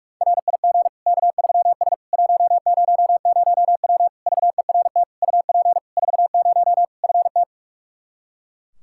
Morse code, heard as a keyed tone, about 46 wpm, 700 Hz. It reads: RIG K3S 100W VERT UP 40 FT
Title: K1USN SST QSO Practice 45wpm